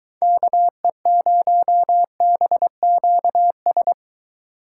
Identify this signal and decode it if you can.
Morse code, heard as a keyed tone, about 22 wpm, 700 Hz. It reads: KE0BQS